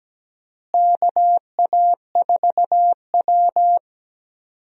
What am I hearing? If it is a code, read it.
Morse code, heard as a keyed tone, about 17 wpm, 700 Hz. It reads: KA4W
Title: USA Call Signs - 4 Characters - Repeated 3x - 17wpm